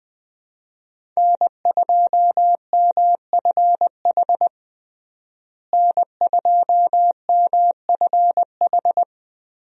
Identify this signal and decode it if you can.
Morse code, heard as a keyed tone, about 20 wpm, 700 Hz. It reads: N2MFH N2MFH
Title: USA Call Signs - 5 Characters - Repeated 2x - 20wpm